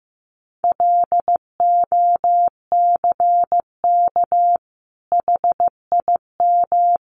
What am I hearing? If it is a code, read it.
Morse code, heard as a keyed tone, about 15 wpm, 700 Hz. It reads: LOCK HIM
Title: Sentences from Top 1300 Words 15wpm